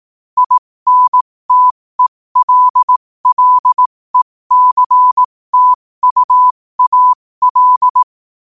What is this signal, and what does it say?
Morse code, heard as a keyed tone, about 18 wpm, spaced as for 15 wpm, 1000 Hz. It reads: INTELLECTUAL
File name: qotdfiles / 15wpm